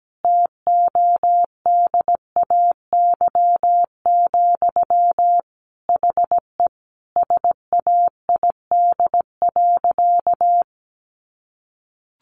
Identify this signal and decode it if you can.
Morse code, heard as a keyed tone, about 17 wpm, 700 Hz. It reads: TODAY, HE SAID.
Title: Sentences from Top 600 Words 17wpm